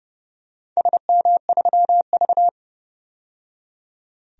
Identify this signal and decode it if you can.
Morse code, heard as a keyed tone, about 30 wpm, 700 Hz. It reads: SM3V
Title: DXCC-100 Call Signs - 4 Characters - 30wpm